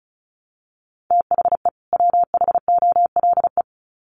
Morse code, heard as a keyed tone, about 34 wpm, 700 Hz. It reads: THE WHOLE
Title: Sets of 2 Words 35wpm